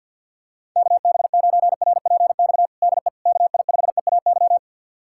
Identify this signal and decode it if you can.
Morse code, heard as a keyed tone, about 52 wpm, 700 Hz. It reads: KB9RPX DE KI5EAY
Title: USA Callsign DE Callsign - 6 Characters - 50wpm